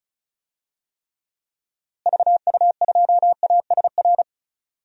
Morse code, heard as a keyed tone, about 36 wpm, 700 Hz. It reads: VU2ASR
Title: DXCC-100 Call Signs - 6 Characters - Repeated 2x - 35wpm